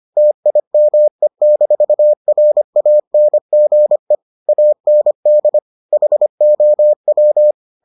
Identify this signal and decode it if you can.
Morse code, heard as a keyed tone, about 25 wpm, 600 Hz. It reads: TIME-RANGE AND HOW